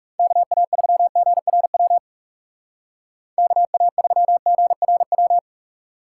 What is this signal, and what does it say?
Morse code, heard as a keyed tone, about 40 wpm, 700 Hz. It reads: KA3GRW KA3GRW